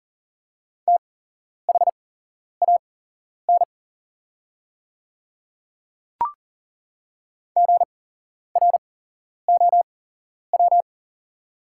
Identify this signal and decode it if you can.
Morse code, heard as a keyed tone, about 40 wpm, spaced as for 11 wpm, 700 Hz. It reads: THAN GROW
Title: 8x Character Spacing - Top 500 Words - 40wpm